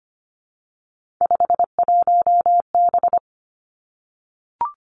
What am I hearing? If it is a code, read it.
Morse code, heard as a keyed tone, about 25 wpm, 700 Hz. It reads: 51B
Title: Three Challenging Letter-Number Combinations 25wpm